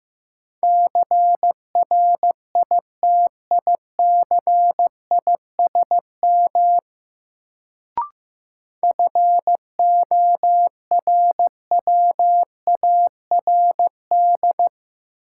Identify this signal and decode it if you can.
Morse code, heard as a keyed tone, about 15 wpm, 700 Hz. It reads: CRITICISM FORWARD
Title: Sentences from Top 1300 Words - New Word Drill Alt - 15wpm